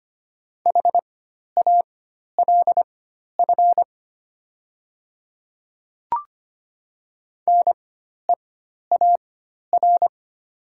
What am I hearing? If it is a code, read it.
Morse code, heard as a keyed tone, about 25 wpm, spaced as for 12 wpm, 700 Hz. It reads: HALF NEAR
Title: 4x Character Spacing - Top 500 Words - 25wpm